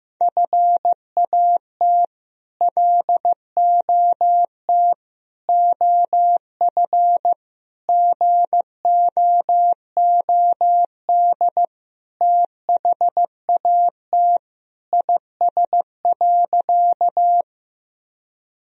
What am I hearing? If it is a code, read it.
Morse code, heard as a keyed tone, about 16 wpm, 700 Hz. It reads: FAT LOT OF GOOD THAT IS.
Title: Sentences from Top 1000 Words - Review Part 4 - 15wpm